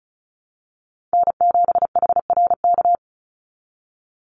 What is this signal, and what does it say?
Morse code, heard as a keyed tone, about 34 wpm, 700 Hz. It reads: N7HRK
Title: Call Signs Repeated 3x - 35wpm